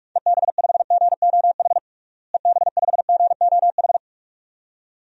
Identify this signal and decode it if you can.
Morse code, heard as a keyed tone, about 46 wpm, 700 Hz. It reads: EB5GOH EB5GOH
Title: DXCC-100 Call Signs - 6 Characters - Repeated 2x - 45wpm